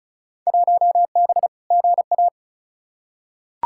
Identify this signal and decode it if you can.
Morse code, heard as a keyed tone, about 36 wpm, 700 Hz. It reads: 1B GA
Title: Field Day Exchanges Encore - 35wpm